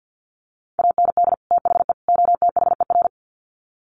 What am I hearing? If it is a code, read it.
Morse code, heard as a keyed tone, about 50 wpm, 700 Hz. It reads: AND THE OTHER
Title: Sets of 3 Words 50wpm